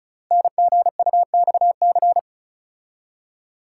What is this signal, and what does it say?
Morse code, heard as a keyed tone, about 36 wpm, 700 Hz. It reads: NGUXC
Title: Code Groups - 5 Letter:Numbers - 35wpm